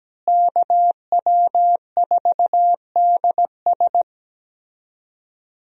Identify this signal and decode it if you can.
Morse code, heard as a keyed tone, about 17 wpm, 700 Hz. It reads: KW4DS
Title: USA Call Signs - 5 Characters - Repeated 2x - 17wpm